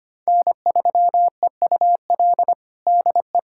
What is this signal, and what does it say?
Morse code, heard as a keyed tone, about 25 wpm, 700 Hz. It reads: N3EUL DE
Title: USA Callsign DE Callsign - 5 Characters - 25wpm